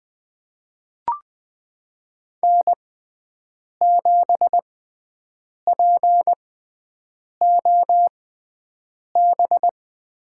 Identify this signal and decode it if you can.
Morse code, heard as a keyed tone, about 20 wpm, spaced as for 7 wpm, 700 Hz. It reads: N7POB
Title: USA Call Signs - 5 Characters - 6x Character Spacing - 20wpm